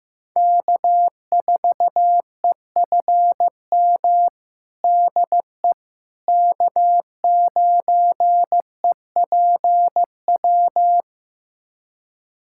Morse code, heard as a keyed tone, about 15 wpm, 700 Hz. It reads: K4EFM DE K9EPW
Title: USA Callsign DE Callsign - 5 Characters - 15wpm